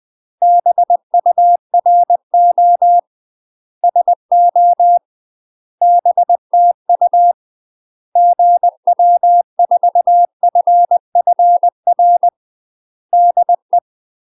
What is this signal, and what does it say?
Morse code, heard as a keyed tone, about 20 wpm, 700 Hz. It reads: BURO SO BTU GW4FFR DE
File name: UK MASTER QSOs TRAINING / 20WPM / UK MASTER QSOs 0034